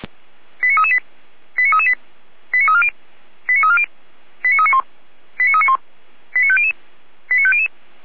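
Five-tone selective call (ZVEI), heard as quick runs of five tones, tones of about 70 ms. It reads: 89338 89338 89340 89340 89391 89391 89500 89500
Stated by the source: ZVEI five-tone selective call 89337 received by SDR, from feuerwehr-orlamuende.org fire brigade recordings.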